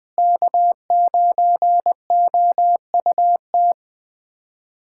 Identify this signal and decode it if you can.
Morse code, heard as a keyed tone, about 21 wpm, 700 Hz. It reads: K9OUT
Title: USA Callsign DE Callsign - 5 Characters - 20wpm